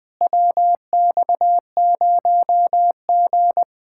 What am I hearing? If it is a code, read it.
Morse code, heard as a keyed tone, about 20 wpm, 700 Hz. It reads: WX0G
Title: Speed-Racing Common QSO Phrases - 17wpm